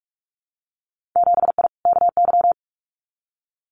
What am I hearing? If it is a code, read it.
Morse code, heard as a keyed tone, about 46 wpm, 700 Hz. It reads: 7I KY
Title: Winter Field Day Exchanges Encore 45wpm